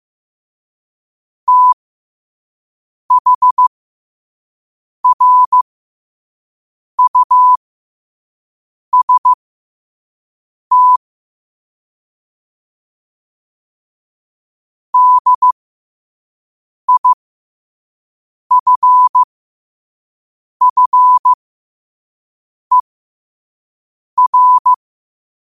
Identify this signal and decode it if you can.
Morse code, heard as a keyed tone, about 15 wpm, spaced as for 5 wpm, 1000 Hz. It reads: THRUST DIFFER
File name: qotdfiles / 5wpm